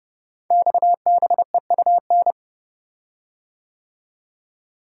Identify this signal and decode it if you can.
Morse code, heard as a keyed tone, about 30 wpm, 700 Hz. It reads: XBEUN
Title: Code Groups - 5 Letters - 30wpm